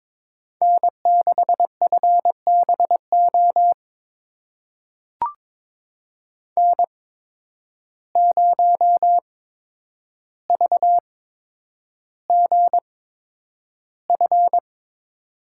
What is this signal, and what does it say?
Morse code, heard as a keyed tone, about 22 wpm, spaced as for 9 wpm, 700 Hz. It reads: N6FBO N0VGF
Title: USA Call Signs - 5 Characters - 8x Character Spacing - 22wpm